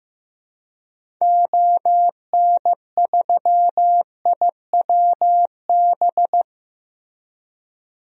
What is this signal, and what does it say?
Morse code, heard as a keyed tone, about 15 wpm, 700 Hz. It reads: ON3IWB
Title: DXCC-100 Call Signs - 6 Characters - Repeated 3x - 15wpm